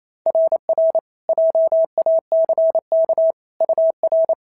Morse code, heard as a keyed tone, about 25 wpm, 650 Hz. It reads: RR JACK UR
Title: SOTA QSO Practice 28wpm